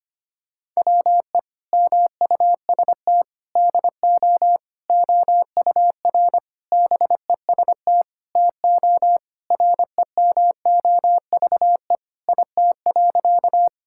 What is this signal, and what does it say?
Morse code, heard as a keyed tone, about 25 wpm, 700 Hz. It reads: WE MUST DO OUR BEST TO REMOVE IT.
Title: Sentences from Top 1000 Words - Review Part 5 - 25wpm